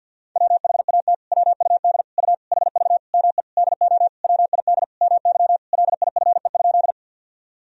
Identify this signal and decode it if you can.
Morse code, heard as a keyed tone, about 50 wpm, 700 Hz. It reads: WHAT WUD U HV ME DO WID MY LIFE?
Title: Sentences from Top 300 Words and Abbreviations 50wpm